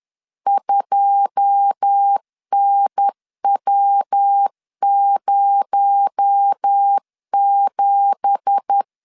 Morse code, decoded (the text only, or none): 2NW07